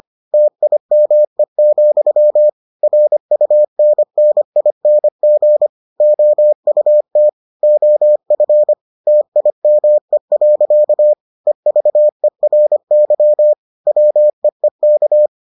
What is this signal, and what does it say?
Morse code, heard as a keyed tone, about 25 wpm, 600 Hz. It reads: TIME, RUNNING OUT OF TIME. EVERY WEEK